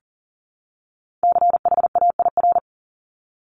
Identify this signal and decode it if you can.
Morse code, heard as a keyed tone, about 38 wpm, 700 Hz. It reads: CHAIR
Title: Top 1751-2000 Words 40wpm